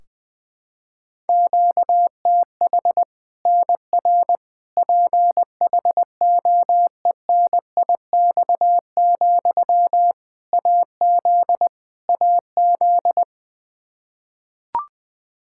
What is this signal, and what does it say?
Morse code, heard as a keyed tone, about 20 wpm, 700 Hz. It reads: QTH NR PHOENIX, AZ AZ
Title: K1USN SST QSO Practice 20wpm